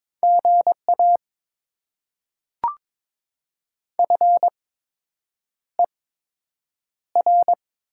Morse code, heard as a keyed tone, about 22 wpm, spaced as for 8 wpm, 700 Hz. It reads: GA FER
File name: Top 20 QSO Elements - 8x Character Spacing - 22wpm